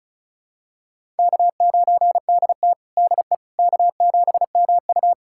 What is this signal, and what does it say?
Morse code, heard as a keyed tone, about 36 wpm, 700 Hz. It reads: K9DT DE K7MU
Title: USA Callsign DE Callsign - 4 Characters - 35wpm